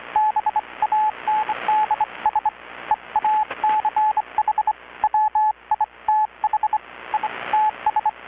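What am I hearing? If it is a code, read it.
Morse code, heard as a keyed tone, about 25 wpm, spaced as for 22 wpm, 840 Hz, with radio noise. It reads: BANDS EACH WITH ITS
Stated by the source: Maritime radiotelegraphy coast station event transmission, received in Chiapas, Mexico about 3,800 km away on a Kenwood R-600 receiver with a 10-metre grounded longwire antenna.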